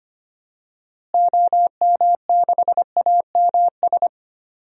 Morse code, heard as a keyed tone, about 25 wpm, 700 Hz. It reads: OM6AMS
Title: DXCC-100 Call Signs - 6 Characters - Repeated 2x - 25wpm